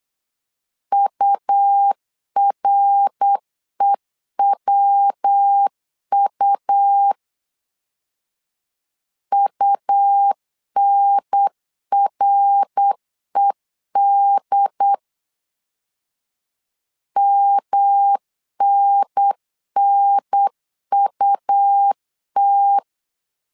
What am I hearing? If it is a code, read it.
Morse code, heard as a keyed tone, about 8 wpm, 800 Hz. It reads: UREWU UNRED MNNUT
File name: LEZIONI / 3 MWRO / 3 MWR 8wpm 2444